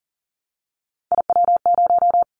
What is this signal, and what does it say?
Morse code, heard as a keyed tone, about 38 wpm, 700 Hz. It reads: IW0
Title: Callsign Prefixes for Top 50 DXCC - 40wpm